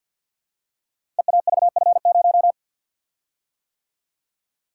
Morse code, heard as a keyed tone, about 50 wpm, 690 Hz. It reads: EAVF0
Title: Code Groups - 5 Letter:Numbers - 50wpm